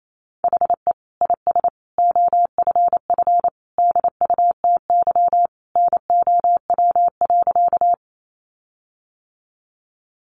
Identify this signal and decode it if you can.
Morse code, heard as a keyed tone, about 28 wpm, 700 Hz. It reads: HE IS OFF DUTY NOW.
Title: Sentences from Top 1500 Words 28wpm